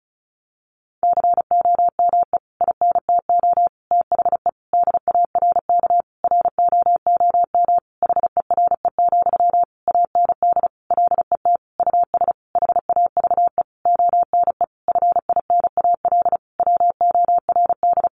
Morse code, heard as a keyed tone, about 34 wpm, 700 Hz. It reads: COME INTO THE DARK ROOM HERE, AND LET US HAVE ONE FINAL WORD
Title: Sentences from Top 1000 Words 35wpm